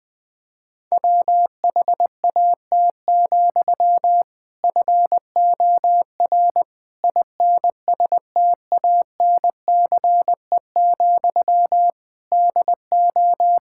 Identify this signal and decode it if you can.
Morse code, heard as a keyed tone, about 20 wpm, 700 Hz. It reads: WHAT, FOR INSTANCE, DO
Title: Sentences from Top 1000 Words - Review Part 5 - 20wpm